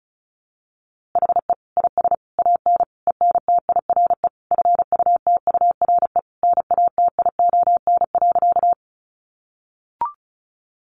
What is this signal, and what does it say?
Morse code, heard as a keyed tone, about 34 wpm, 700 Hz. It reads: HE IS AN ENTIRE FUTURE NATION.
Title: A Sound Of Thunder 35wpm